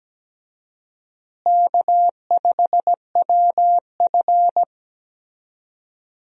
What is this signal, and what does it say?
Morse code, heard as a keyed tone, about 17 wpm, 700 Hz. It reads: K5WF
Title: USA Call Signs - 4 Characters - Repeated 3x - 17wpm